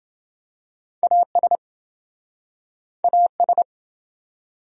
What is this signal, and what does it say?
Morse code, heard as a keyed tone, about 28 wpm, 700 Hz. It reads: AS AS